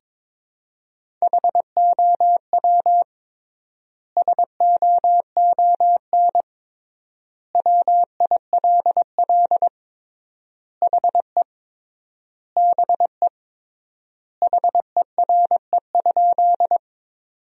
Morse code, heard as a keyed tone, about 21 wpm, 700 Hz. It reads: HOW SOON WILL HE BE HERE?
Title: 3x Word-Spacing - Sentences from Top 400 Words - 22wpm